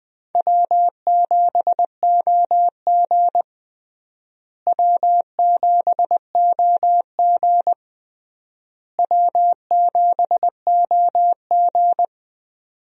Morse code, heard as a keyed tone, about 20 wpm, 700 Hz. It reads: W7OG W7OG W7OG
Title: USA Call Signs - 4 Characters - Repeated 3x - 20wpm